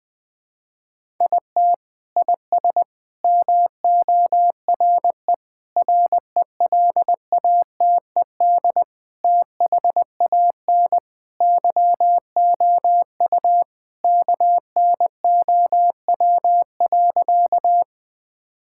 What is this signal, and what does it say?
Morse code, heard as a keyed tone, about 20 wpm, 700 Hz. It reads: IT IS MORE RELATED THAN YOU KNOW.
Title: Sentences from Top 1000 Words - Review Part 5 - 20wpm